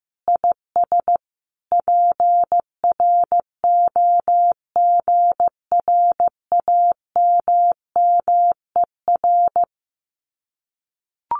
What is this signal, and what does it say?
Morse code, heard as a keyed tone, about 15 wpm, 700 Hz. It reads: IS PROGRAMMER
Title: Common QSO Phrases Encore 15wpm